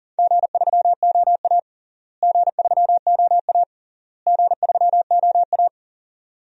Morse code, heard as a keyed tone, about 40 wpm, 700 Hz. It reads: G3OA G3OA G3OA
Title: DXCC-100 Call Signs - 4 Characters - Repeated 3x - 40wpm